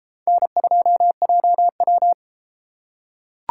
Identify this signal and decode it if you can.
Morse code, heard as a keyed tone, about 31 wpm, 700 Hz. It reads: N2JW